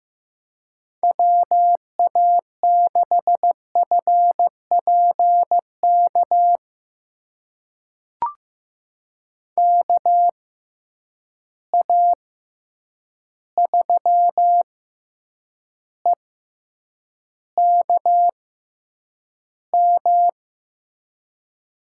Morse code, heard as a keyed tone, about 15 wpm, spaced as for 8 wpm, 700 Hz. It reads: WA6FPK KA3EKM
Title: USA Call Signs - 6 Characters - 6x Character Spacing - 15wpm